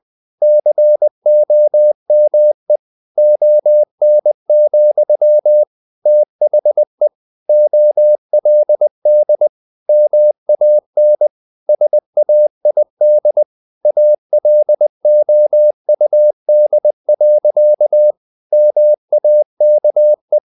Morse code, heard as a keyed tone, about 20 wpm, 600 Hz. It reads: COME ON, THE OLD MAN SAID ALOUD. MAKE
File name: the old man and the sea / the old man and the sea part 1 20wpm